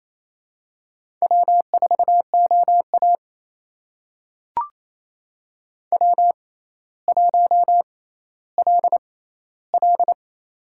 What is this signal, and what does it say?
Morse code, heard as a keyed tone, about 28 wpm, spaced as for 14 wpm, 700 Hz. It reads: W4OA W1LL
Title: USA Call Signs - 6x Character Spacing - 28wpm